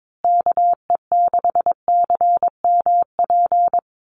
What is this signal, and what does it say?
Morse code, heard as a keyed tone, about 22 wpm, 700 Hz. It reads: KE6CMP
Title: USA Call Signs Repeated 3x 22wpm